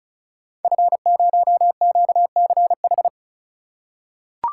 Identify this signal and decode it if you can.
Morse code, heard as a keyed tone, about 35 wpm, 700 Hz. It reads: F0QCH
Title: Code Groups - 5 Letter:Numbers - 35wpm